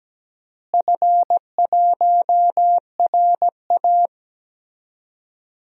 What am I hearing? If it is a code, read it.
Morse code, heard as a keyed tone, about 17 wpm, 700 Hz. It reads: F1RA
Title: DXCC-100 Call Signs - 4 Characters - Repeated 2x - 17wpm